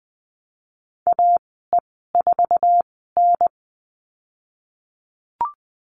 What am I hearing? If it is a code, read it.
Morse code, heard as a keyed tone, about 20 wpm, spaced as for 14 wpm, 700 Hz. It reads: AE4N
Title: 154 Lesson 25g - Letter p - Call Signs 20wpm